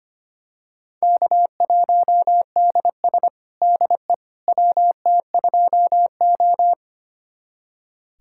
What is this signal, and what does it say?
Morse code, heard as a keyed tone, about 25 wpm, 700 Hz. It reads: K1DS DE WT2O
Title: USA Callsign DE Callsign - 4 Characters - 25wpm